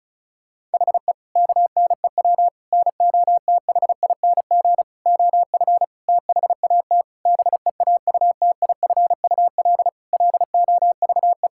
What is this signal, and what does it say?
Morse code, heard as a keyed tone, about 34 wpm, 700 Hz. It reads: HE KNEW NOTHING OF THAT BEAUTIFUL LOVE
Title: Sentences from Top 1600 Words 35wpm